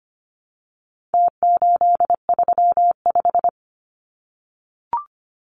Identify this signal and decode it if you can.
Morse code, heard as a keyed tone, about 25 wpm, 700 Hz. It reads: T835
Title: Numbers - Contest Style - 4-Digit 25wpm